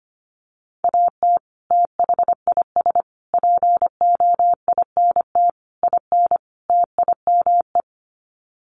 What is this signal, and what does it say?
Morse code, heard as a keyed tone, about 25 wpm, 700 Hz. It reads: AT THIS POINT IN TIME
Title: Sets of 5 Words Encore 25wpm